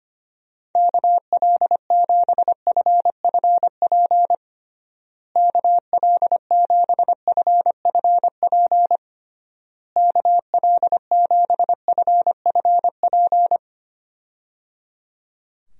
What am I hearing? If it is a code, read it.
Morse code, heard as a keyed tone, about 25 wpm, 700 Hz. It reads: KL7FFP KL7FFP KL7FFP
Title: USA Call Signs - 6 Characters - Repeated 3x - 25wpm